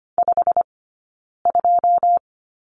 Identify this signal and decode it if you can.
Morse code, heard as a keyed tone, about 25 wpm, spaced as for 9 wpm, 690 Hz. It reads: 52
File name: Lesson 15d - Number 5 - Two Characters 25wpm